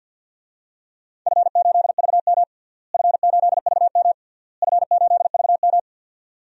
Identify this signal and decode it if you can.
Morse code, heard as a keyed tone, about 50 wpm, 700 Hz. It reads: F8VM F8VM F8VM
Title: DXCC-100 Call Signs - 4 Characters - Repeated 3x - 50wpm